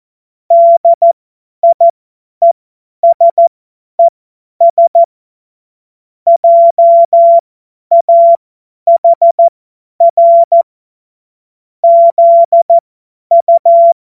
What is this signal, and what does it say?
Morse code, heard as a keyed tone, about 14 wpm, spaced as for 10 wpm, 680 Hz. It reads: DIESES JAHR ZU
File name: Morsekurs / Witze / Witz 31. Januar 2023 50 F70